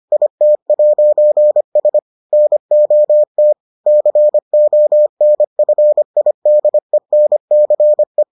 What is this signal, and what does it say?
Morse code, heard as a keyed tone, about 25 wpm, 600 Hz. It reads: IT'S NOT CONFIDENCE